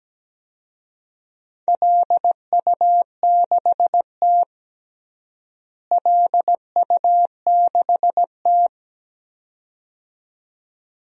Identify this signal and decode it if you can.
Morse code, heard as a keyed tone, about 17 wpm, 700 Hz. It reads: LU6T LU6T
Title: DXCC-100 Call Signs - 4 Characters - Repeated 2x - 17wpm